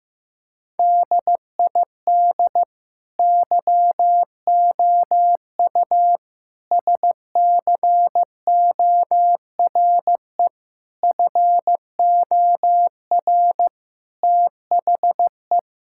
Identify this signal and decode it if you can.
Morse code, heard as a keyed tone, about 15 wpm, 700 Hz. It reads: DID YOU SCORE FOR THE
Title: Sentences from Top 1600 Words 15wpm